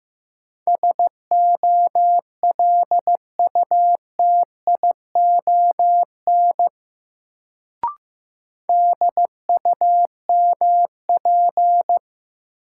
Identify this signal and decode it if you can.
Morse code, heard as a keyed tone, about 15 wpm, 700 Hz. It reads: SOLUTION DUMP
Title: Sentences from Top 1300 Words - New Word Drill Alt - 15wpm